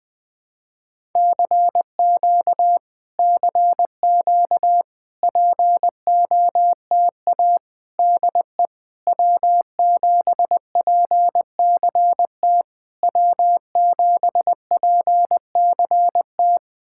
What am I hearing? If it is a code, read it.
Morse code, heard as a keyed tone, about 20 wpm, 700 Hz. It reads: CQ CQ POTA DE W7PCT W7PCT
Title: POTA QSO Practice - 20wpm